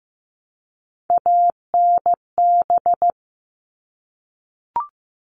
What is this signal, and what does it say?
Morse code, heard as a keyed tone, about 15 wpm, 700 Hz. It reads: ANB
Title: All Cut Numbers - 3-Digit 15wpm